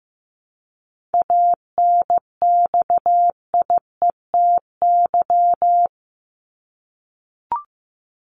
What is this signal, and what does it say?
Morse code, heard as a keyed tone, about 15 wpm, 700 Hz. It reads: ANXIETY
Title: Top 1501-1750 Words 15wpm